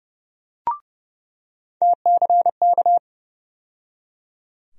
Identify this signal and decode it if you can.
Morse code, heard as a keyed tone, about 30 wpm, 700 Hz. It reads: TCK
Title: USA Call Sign Postfixes - 3 Letters 30wpm